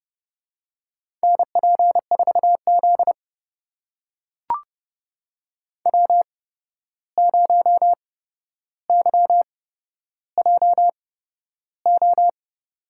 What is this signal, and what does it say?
Morse code, heard as a keyed tone, about 31 wpm, spaced as for 12 wpm, 700 Hz. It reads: NP4Z W0YJO
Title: USA Call Signs - 8x Character Spacing - 30wpm